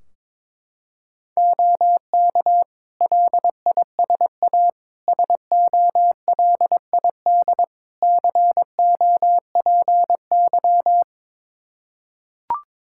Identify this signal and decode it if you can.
Morse code, heard as a keyed tone, about 22 wpm, 700 Hz. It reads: OK LISA SOLID COPY
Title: Practice QSO - Exchange 3 - More Info - 22wpm